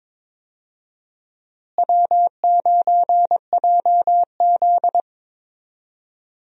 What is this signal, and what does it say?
Morse code, heard as a keyed tone, about 22 wpm, 700 Hz. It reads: W9JZ
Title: USA Call Signs - 4 Characters - Repeated 2x - 22wpm